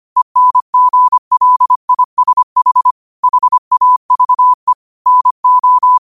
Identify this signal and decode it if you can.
Morse code, heard as a keyed tone, about 25 wpm, 1000 Hz. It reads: ENGLISH HAVE NO
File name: qotdfiles / 25wpm